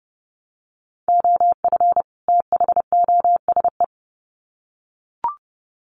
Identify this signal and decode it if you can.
Morse code, heard as a keyed tone, about 30 wpm, 700 Hz. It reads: OF THOSE
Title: Sets of 2 Words 30wpm